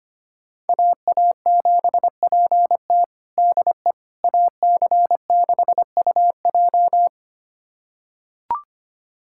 Morse code, heard as a keyed tone, about 25 wpm, 700 Hz. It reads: AA7PT DE AC6UJ
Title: USA Callsign DE Callsign - 5 Characters - 25wpm